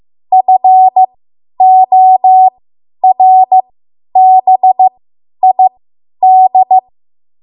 Morse code, heard as a keyed tone, about 15 wpm, spaced as for 10 wpm, 750 Hz. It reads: FORBID
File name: Archive / 10 WPM / 10WPM